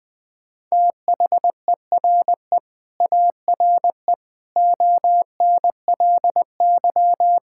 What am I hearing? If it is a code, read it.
Morse code, heard as a keyed tone, about 21 wpm, 700 Hz. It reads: THERE ARE ONLY
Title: Sentences with 9-Word Phrases - 20wpm